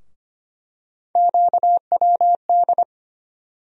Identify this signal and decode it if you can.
Morse code, heard as a keyed tone, about 25 wpm, 700 Hz. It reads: QWD
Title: USA Call Sign Postfixes - 3 Letters 25wpm